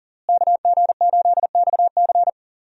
Morse code, heard as a keyed tone, about 37 wpm, 700 Hz. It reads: KG8XC
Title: USA Call Signs - 5 Characters - Repeated 3x - 40wpm